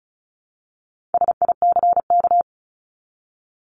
Morse code, heard as a keyed tone, about 34 wpm, 700 Hz. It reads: SICK
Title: Top 1501-1750 Words 35wpm